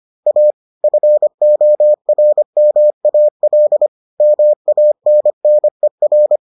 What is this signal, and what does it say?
Morse code, heard as a keyed tone, about 25 wpm, 600 Hz. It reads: A FORMAL MANNER